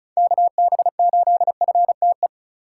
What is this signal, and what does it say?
Morse code, heard as a keyed tone, about 34 wpm, 700 Hz. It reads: KB8FTE
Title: USA Call Signs - 6 Characters - 4x Character Spacing - 35wpm